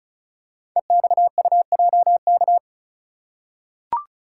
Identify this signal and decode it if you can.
Morse code, heard as a keyed tone, about 36 wpm, 700 Hz. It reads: EXUJK
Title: Code Groups - 5 Letters - 35wpm